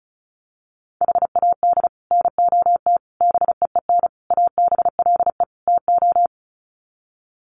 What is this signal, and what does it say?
Morse code, heard as a keyed tone, about 34 wpm, 700 Hz. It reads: HAD NOT BEEN ABLE TO
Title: Sets of 5 Words Encore 35wpm